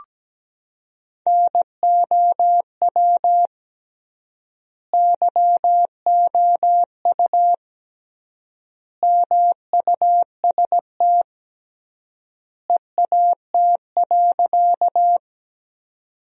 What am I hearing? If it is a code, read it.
Morse code, heard as a keyed tone, about 17 wpm, 700 Hz. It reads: NOW YOU MUST EAT.
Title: 3x Word-Spacing - Sentences from Top 300 Words - 17wpm